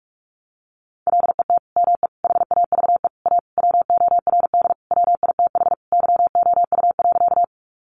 Morse code, heard as a keyed tone, about 44 wpm, 700 Hz. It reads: LET ME HAVE A WORD WITH YOU.
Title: Sentences from Top 300 Words 45wpm